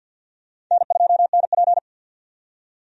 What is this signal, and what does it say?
Morse code, heard as a keyed tone, about 53 wpm, 700 Hz. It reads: N2NP
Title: USA Call Signs - 4 Characters - 50wpm